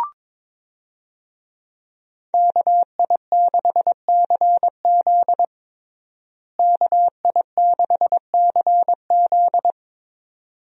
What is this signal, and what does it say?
Morse code, heard as a keyed tone, about 22 wpm, 700 Hz. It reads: KI6CZ KI6CZ
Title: USA Call Signs - 5 Characters - Repeated 2x - 22wpm